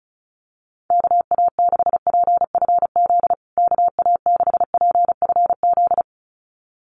Morse code, heard as a keyed tone, about 34 wpm, 700 Hz. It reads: KA6PFZ KA6PFZ
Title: USA Call Signs Repeated 2x 35wpm